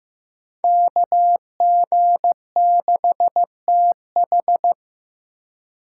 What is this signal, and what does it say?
Morse code, heard as a keyed tone, about 18 wpm, 700 Hz. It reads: KG6TH
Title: Speed Racing - USA 2x2 Call Signs 15wpm